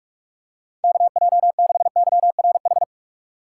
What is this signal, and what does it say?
Morse code, heard as a keyed tone, about 47 wpm, 700 Hz. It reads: KJ6YRH
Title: USA Call Signs - 6 Characters - 4x Character Spacing - 45wpm